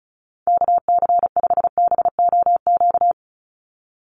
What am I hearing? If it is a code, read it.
Morse code, heard as a keyed tone, about 34 wpm, 700 Hz. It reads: KC5BOQ
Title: USA Call Signs 35wpm